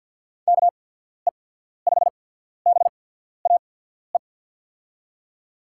Morse code, heard as a keyed tone, about 47 wpm, spaced as for 14 wpm, 700 Hz. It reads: KE5BAE
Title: USA Call Signs - 6 Characters - 8x Character Spacing - 50wpm